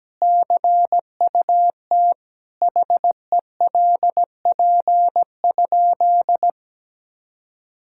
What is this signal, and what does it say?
Morse code, heard as a keyed tone, about 17 wpm, 700 Hz. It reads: CUT HELP?
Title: Sentences from Top 1000 Words - Review Part 5 - 17wpm